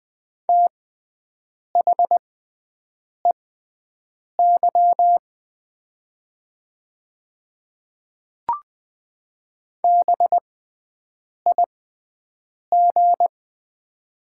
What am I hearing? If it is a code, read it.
Morse code, heard as a keyed tone, about 20 wpm, spaced as for 7 wpm, 700 Hz. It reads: THEY BIG
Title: 6x Character Spacing - Top 200 Words - 20wpm